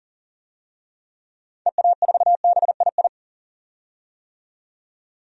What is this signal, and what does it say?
Morse code, heard as a keyed tone, about 40 wpm, 700 Hz. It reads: EA4BII
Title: DXCC-100 Call Signs - 6 Characters - 40wpm